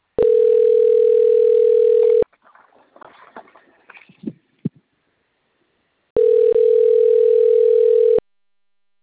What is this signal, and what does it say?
Ringback tone (North America), 440 + 480 Hz.
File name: rg-1600-unknown- - - .17088